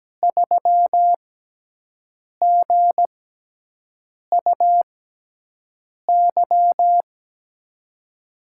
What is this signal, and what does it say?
Morse code, heard as a keyed tone, about 17 wpm, spaced as for 6 wpm, 700 Hz. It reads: 3GUY
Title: USA Call Signs - 6x Character Spacing - 17wpm